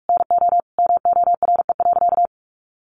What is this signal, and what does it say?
Morse code, heard as a keyed tone, about 44 wpm, 700 Hz. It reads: NO MORE.
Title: Sentences from Top 800 Words 45wpm